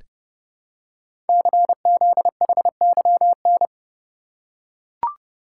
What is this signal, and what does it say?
Morse code, heard as a keyed tone, about 30 wpm, 700 Hz. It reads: CZHYN
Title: Code Groups - 5 Letters - 30wpm